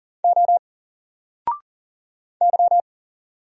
Morse code, heard as a keyed tone, about 40 wpm, 700 Hz. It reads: O Y
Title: Single Letter-Number Alt 40wpm